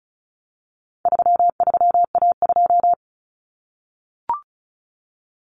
Morse code, heard as a keyed tone, about 34 wpm, 700 Hz. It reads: 33A2
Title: Common Cut Numbers - 4-Digit 35wpm